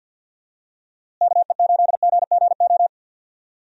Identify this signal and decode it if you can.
Morse code, heard as a keyed tone, about 49 wpm, 700 Hz. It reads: KE8GGO
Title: USA Call Signs - 6 Characters - Repeated 2x - 50wpm